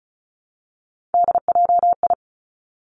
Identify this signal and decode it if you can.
Morse code, heard as a keyed tone, about 36 wpm, 700 Hz. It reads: DJI
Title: Three Challenging Letter-Number Combinations 35wpm